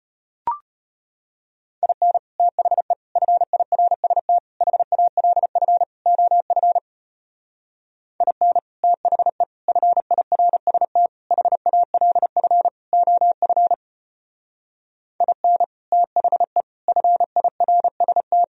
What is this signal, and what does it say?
Morse code, heard as a keyed tone, about 33 wpm, 700 Hz. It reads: IN THE FIRST HALF OF IN THE FIRST HALF OF IN THE FIRST